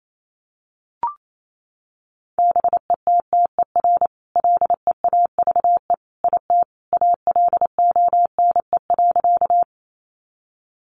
Morse code, heard as a keyed tone, about 28 wpm, 700 Hz. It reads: BETTER LEAVE IT ALONE.
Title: Sentences from Top 1000 Words 28wpm